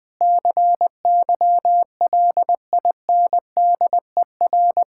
Morse code, heard as a keyed tone, about 20 wpm, 700 Hz. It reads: CYLINDER
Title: Difficult Words - Speed-Racing with Hints - 20wpm